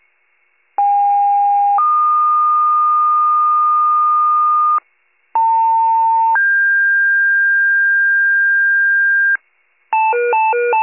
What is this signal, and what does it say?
Two-tone page, heard as a one-second tone, then a three-second tone, about 810 then 1220 Hz, about 900 then 1600 Hz, with noise. It is heard from 0.8 to 9.4 s.